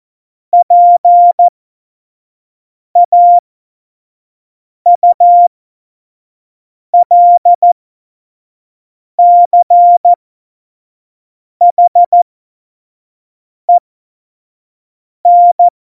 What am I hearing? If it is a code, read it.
Morse code, heard as a keyed tone, about 14 wpm, spaced as for 5 wpm, 700 Hz. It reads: PAULCHEN
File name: Morsekurs / Witze / Witz 26. Januar 2021, 25 F70